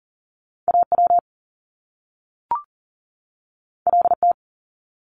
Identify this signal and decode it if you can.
Morse code, heard as a keyed tone, about 41 wpm, 700 Hz. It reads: AW LT
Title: Top Two Letter Combinations in Words 40wpm